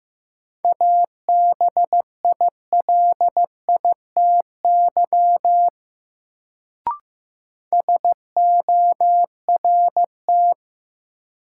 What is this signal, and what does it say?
Morse code, heard as a keyed tone, about 15 wpm, 700 Hz. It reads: ABILITY SORT
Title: Sentences from Top 1000 Words - Review Part 3 - New Word Drill - 15wpm